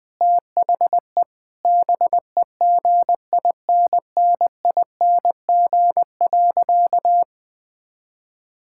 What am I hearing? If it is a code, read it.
Morse code, heard as a keyed tone, about 21 wpm, 700 Hz. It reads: THE BEGINNING.
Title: Sentences from Top 1000 Words - Review Part 5 - 20wpm